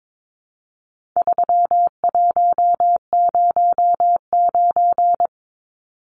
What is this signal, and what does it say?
Morse code, heard as a keyed tone, about 22 wpm, 700 Hz. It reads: 3109
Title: Numbers - 4-Digit 22wpm